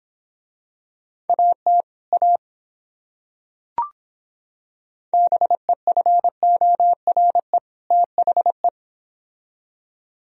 Morse code, heard as a keyed tone, about 26 wpm, 700 Hz. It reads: AT A BEFORE THE